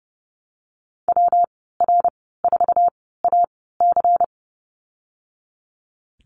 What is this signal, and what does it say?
Morse code, heard as a keyed tone, about 30 wpm, spaced as for 17 wpm, 700 Hz. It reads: WR4AC
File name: Lesson 19g - Letter w - Call Signs 30wpm